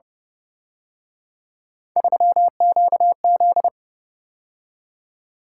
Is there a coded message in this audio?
Yes: Morse code from 2.0 to 3.7 s, heard as a keyed tone, about 30 wpm, 700 Hz. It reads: 3QZ